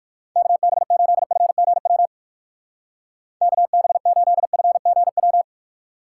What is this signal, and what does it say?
Morse code, heard as a keyed tone, about 63 wpm, 700 Hz. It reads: KB8FGW KB8FGW